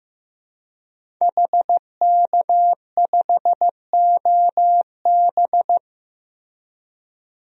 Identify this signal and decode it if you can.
Morse code, heard as a keyed tone, about 15 wpm, 700 Hz. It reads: HK5OB
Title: DXCC-100 Call Signs - 5 Characters - Repeated 2x - 15wpm